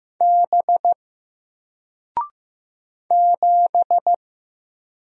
Morse code, heard as a keyed tone, about 15 wpm, 700 Hz. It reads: B 7
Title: Single Letter-Number 15wpm